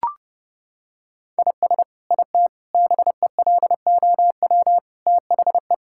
Morse code, heard as a keyed tone, about 28 wpm, 700 Hz. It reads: IS IT BELOW THE
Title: Sentences from Top 1000 Words - Review Part 5 - 30wpm